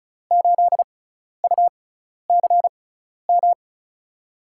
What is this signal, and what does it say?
Morse code, heard as a keyed tone, about 36 wpm, spaced as for 12 wpm, 700 Hz. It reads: 8UCM
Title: USA Call Signs - 6x Character Spacing - 35wpm